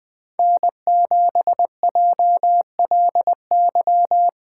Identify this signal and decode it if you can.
Morse code, heard as a keyed tone, about 20 wpm, 700 Hz. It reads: N7JLY
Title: USA Call Signs - 8x Character Spacing - 20wpm